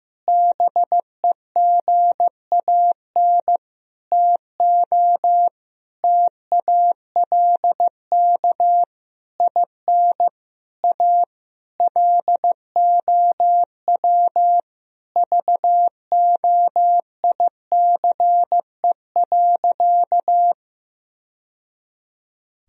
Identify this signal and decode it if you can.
Morse code, heard as a keyed tone, about 15 wpm, 700 Hz. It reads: BEGAN TO TALK IN A LOW VOICE.
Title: Sentences from Top 1000 Words - Review Part 1 - 15wpm